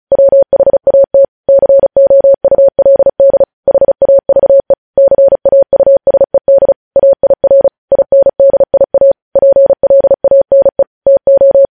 Morse code, heard as a keyed tone, about 36 wpm, 550 Hz. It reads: WHAT COULD HAVE CAUSED AIR INDIA PLANE TO
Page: news headlines in morse code at 35 wpm